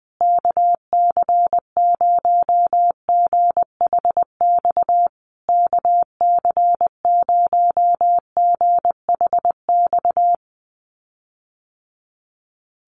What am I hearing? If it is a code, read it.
Morse code, heard as a keyed tone, about 20 wpm, 700 Hz. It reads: KC0GHX KC0GHX
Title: USA Call Signs Repeated 3x 20wpm